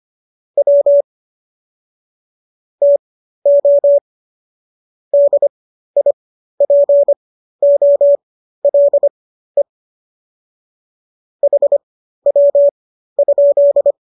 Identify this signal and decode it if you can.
Morse code, heard as a keyed tone, about 25 wpm, spaced as for 13 wpm, 570 Hz. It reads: W TO DIPOLE HW?